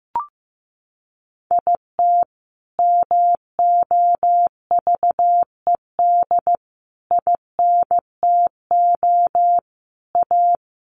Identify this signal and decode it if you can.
Morse code, heard as a keyed tone, about 15 wpm, 700 Hz. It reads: IT MOVED INTO A
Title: A Sound Of Thunder 15wpm